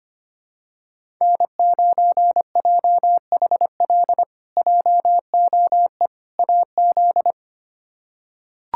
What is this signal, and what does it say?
Morse code, heard as a keyed tone, about 25 wpm, 700 Hz. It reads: N9JHL JOE AZ
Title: K1USN SST QSO Practice - 25wpm